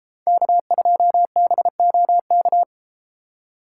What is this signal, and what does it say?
Morse code, heard as a keyed tone, about 33 wpm, 700 Hz. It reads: K2BOK